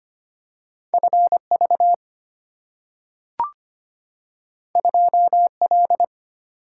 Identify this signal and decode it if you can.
Morse code, heard as a keyed tone, about 25 wpm, 700 Hz. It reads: F V 2 L
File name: Two Letter-Number Combinations - 25wpm